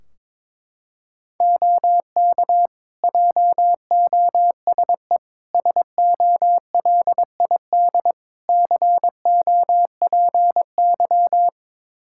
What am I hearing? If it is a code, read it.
Morse code, heard as a keyed tone, about 21 wpm, 700 Hz. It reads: OK JOSE SOLID COPY
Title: Practice QSO - Exchange 3 - More Info - 22wpm